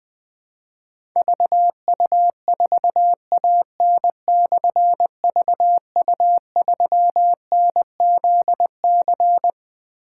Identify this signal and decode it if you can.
Morse code, heard as a keyed tone, about 20 wpm, 700 Hz. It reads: VU4AN/VU3NZC
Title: Difficult Call Signs 20wpm